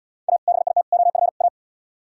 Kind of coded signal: Morse code

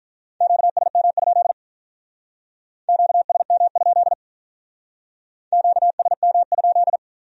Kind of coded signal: Morse code

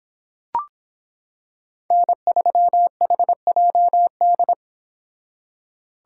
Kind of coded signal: Morse code

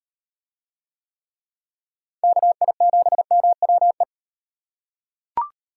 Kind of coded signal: Morse code

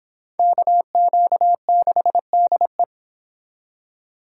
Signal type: Morse code

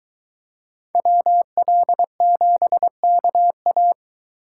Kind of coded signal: Morse code

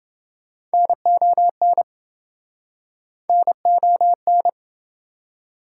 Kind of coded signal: Morse code

